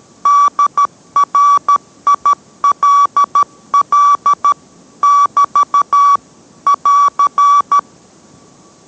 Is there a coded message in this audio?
Yes: Morse code from 0.3 to 7.8 s, heard as a keyed tone, about 13 wpm, 1150 Hz. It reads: DRILL <BT> <AR>